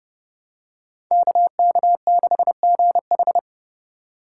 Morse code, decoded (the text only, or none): KK6GH